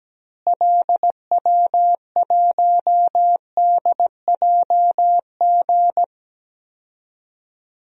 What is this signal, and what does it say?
Morse code, heard as a keyed tone, about 17 wpm, 700 Hz. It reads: LW1DJG
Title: DXCC-100 Call Signs - 6 Characters - Repeated 2x - 17wpm